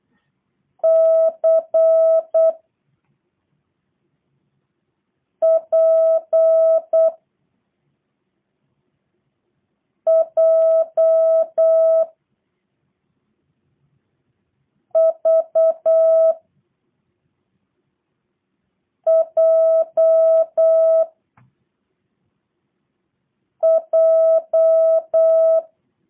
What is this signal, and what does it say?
Morse code, heard as a keyed tone, about 8 wpm, 660 Hz. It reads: C P J V J J
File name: jh8cbh / 2023cw / review1-5